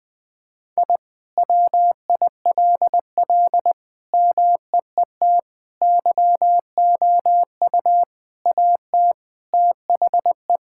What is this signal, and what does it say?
Morse code, heard as a keyed tone, about 21 wpm, 700 Hz. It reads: I WILL MEET YOU AT THE
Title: All-Age Appropriate Jokes - 20wpm